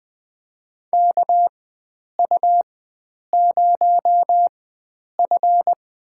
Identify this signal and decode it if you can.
Morse code, heard as a keyed tone, about 20 wpm, spaced as for 9 wpm, 700 Hz. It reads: KU0F
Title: USA Call Signs - 4 Characters - 4x Character Spacing - 20wpm